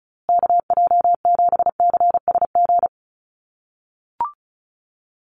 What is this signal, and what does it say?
Morse code, heard as a keyed tone, about 34 wpm, 700 Hz. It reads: KJ7CSG
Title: USA Call Signs Repeated 2x 35wpm